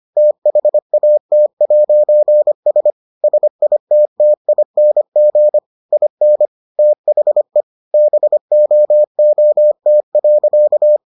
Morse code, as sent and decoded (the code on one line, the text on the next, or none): - .... .- - .----. ... / ... .. - - .. -. --. / .. -. / - .... / -... --- --- - .-.-.-
THAT'S SITTING IN THE BOOT.